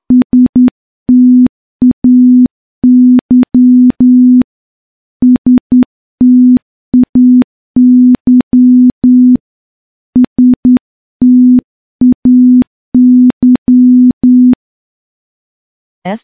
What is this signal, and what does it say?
Morse code, heard as a keyed tone, about 10 wpm, 250 Hz. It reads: STAY STAY STAY
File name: morse / morse-training-10WPM